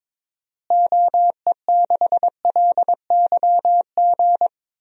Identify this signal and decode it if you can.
Morse code, heard as a keyed tone, about 22 wpm, 700 Hz. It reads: OE6LYG
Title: DXCC-100 Call Signs - 6 Characters - Repeated 2x - 22wpm